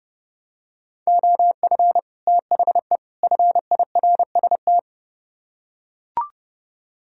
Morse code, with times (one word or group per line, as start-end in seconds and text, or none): 1.1-2.0 OF
2.3-3.0 THE
3.2-4.8 FIRST